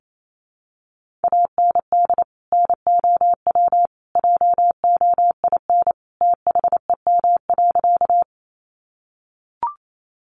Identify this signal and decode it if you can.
Morse code, heard as a keyed tone, about 28 wpm, 700 Hz. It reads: AND NOW JOIN THEM.
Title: Sentences from Top 1400 Words 28wpm